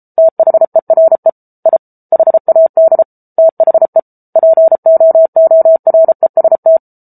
Morse code, heard as a keyed tone, about 34 wpm, 650 Hz. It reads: THERE I HAD THE POOREST